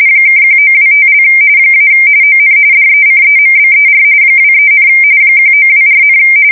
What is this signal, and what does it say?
RTTY, about 45.45 baud. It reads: (1925) - ON DECEMBER 23, 1924,
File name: fsk170 / teletype-corp